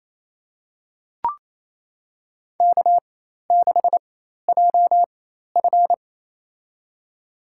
Morse code, heard as a keyed tone, about 28 wpm, spaced as for 13 wpm, 700 Hz. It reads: K6JF
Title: USA Call Signs - 4 Characters - 4x Character Spacing - 28wpm